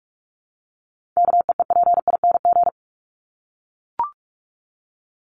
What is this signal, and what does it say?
Morse code, heard as a keyed tone, about 44 wpm, 700 Hz. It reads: KEEPING
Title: ING Suffix - Top 100 Words with Base Words 45wpm